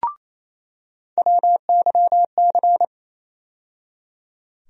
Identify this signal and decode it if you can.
Morse code, heard as a keyed tone, about 28 wpm, 700 Hz. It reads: WYC
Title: USA Call Sign Postfixes - 3 Letters 28wpm